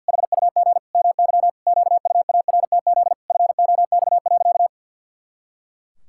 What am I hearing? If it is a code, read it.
Morse code, heard as a keyed tone, about 52 wpm, 700 Hz. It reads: HUG MY QUARTZ FOX.
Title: Pangram Challenge v2 - 50wpm